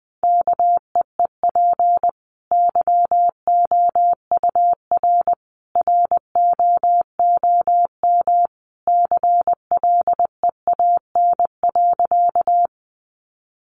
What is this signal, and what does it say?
Morse code, heard as a keyed tone, about 20 wpm, 700 Hz. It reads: KEEP YOUR ROOM CLEAN.
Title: Sentences from Top 1100 Words 20wpm